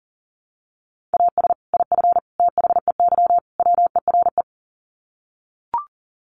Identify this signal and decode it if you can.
Morse code, heard as a keyed tone, about 40 wpm, 700 Hz. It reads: AS IF THEY WERE
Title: Sets of 4 Words 40wpm